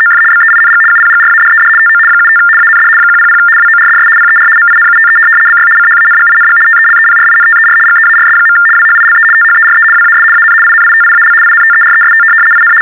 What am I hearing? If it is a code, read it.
RTTY, about 75 baud. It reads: OF BEING UPGRADED TO 75 AND 100 WORDS PER MINUTE BY CHANGING TELEPRINTER GEARS. WHILE THERE WAS AN INITIAL INTEREST IN
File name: teletype-RTTY-75